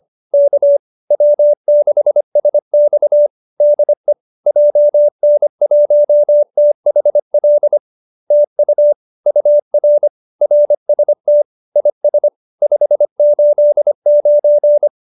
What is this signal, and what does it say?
Morse code, heard as a keyed tone, about 25 wpm, 590 Hz. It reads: K W6SX DE JN1THL TU UR RST IS 589